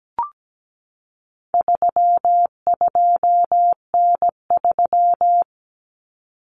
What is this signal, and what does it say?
Morse code, heard as a keyed tone, about 17 wpm, 700 Hz. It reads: 32N3
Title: Common Cut Numbers - 4-Digit 17wpm